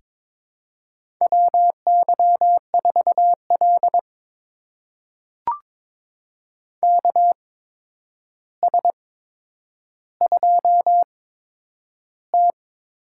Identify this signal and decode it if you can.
Morse code, heard as a keyed tone, about 22 wpm, spaced as for 9 wpm, 700 Hz. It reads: WY4L KS2T
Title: USA Call Signs - 4 Characters - 8x Character Spacing - 22wpm